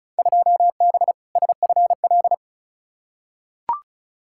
Morse code, heard as a keyed tone, about 34 wpm, 700 Hz. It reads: 2B SFL
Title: Field Day Exchanges - 35wpm